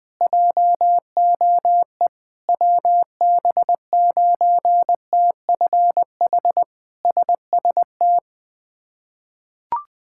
Morse code, heard as a keyed tone, about 20 wpm, 700 Hz. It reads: JOE WB9TFH SST
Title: K1USN SST QSO Practice - 20wpm